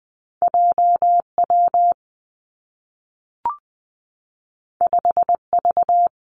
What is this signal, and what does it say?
Morse code, heard as a keyed tone, about 20 wpm, 700 Hz. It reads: J W 5 V
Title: Two Challenging Letter-Number Combinations 20wpm